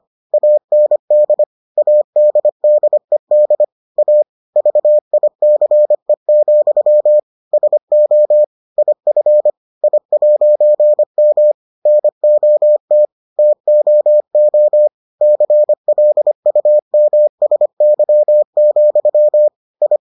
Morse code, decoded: AND ADDED A VICE, SO IF I'M NOT TOO CLUMSY, I

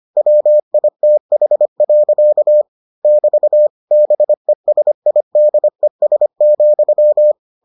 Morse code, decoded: WITH. <BT> BESIDES,

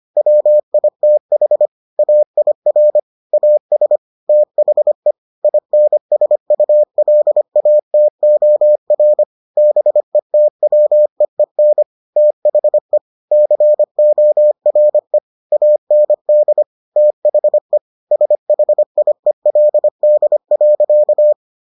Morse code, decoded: WITH AIR AS THE INSULATOR BETWEEN THE CORE AND THE SHIELD.

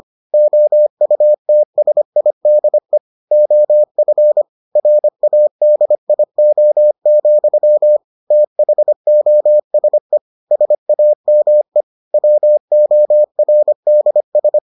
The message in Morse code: OUTSIDE OF RADIO, THOSE SAME WORDS

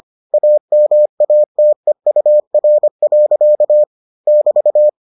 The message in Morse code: AMATEUR. <BT>

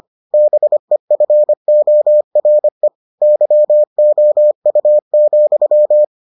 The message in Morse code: BEFORE YOU,

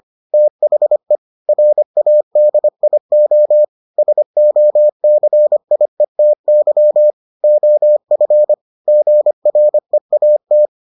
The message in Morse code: THE RADIO SOCIETY OF GREAT

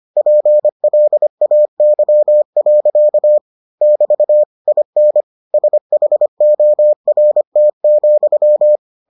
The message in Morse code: PLAY. <BT> IN SHORT,

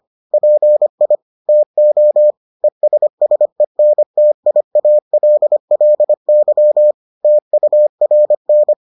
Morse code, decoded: PI TO ESSENTIALLY TURN

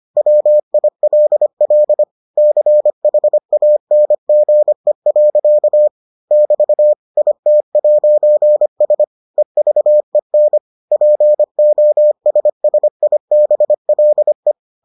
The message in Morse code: WILL CHANGE. <BT> IT'S EVEN POSSIBLE

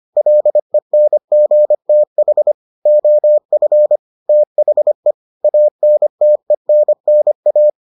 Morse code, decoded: LENGTH OF THE ANTENNA

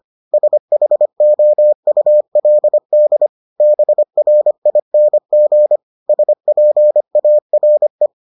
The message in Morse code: SHOULD BRING SPARE

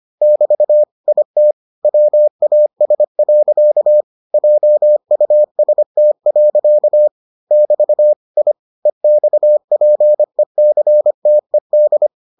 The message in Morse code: <BT> IT WAS. JUST. <BT> I EXPECTED

